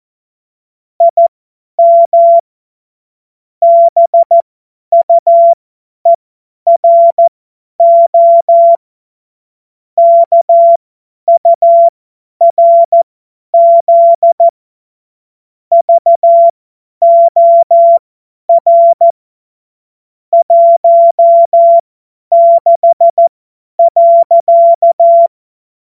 Morse code, heard as a keyed tone, about 14 wpm, spaced as for 10 wpm, 680 Hz. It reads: IM BUERO KURZ VOR 16.
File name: Witz-28-Oktober-2025-50-F70